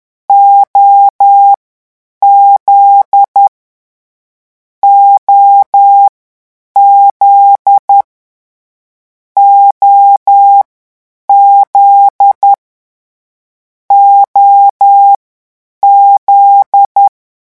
Morse code, heard as a keyed tone, about 10 wpm, 800 Hz. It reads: O Z O Z O Z O Z
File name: morse mp3 / lecon9-vitesse 10